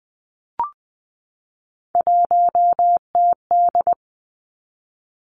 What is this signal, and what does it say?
Morse code, heard as a keyed tone, about 20 wpm, 700 Hz. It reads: 1TD
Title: Three Challenging Letter-Number Combinations 20wpm